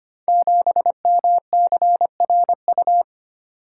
Morse code, heard as a keyed tone, about 25 wpm, 700 Hz. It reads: 7MCRU
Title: Code Groups - 5 Letter:Numbers - 25wpm